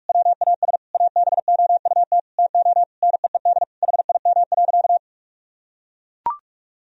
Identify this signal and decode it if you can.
Morse code, heard as a keyed tone, about 41 wpm, 700 Hz. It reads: WAS ABOUT TO NEED HIM.
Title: Sentences from Top 1000 Words - Review Part 1 - 45wpm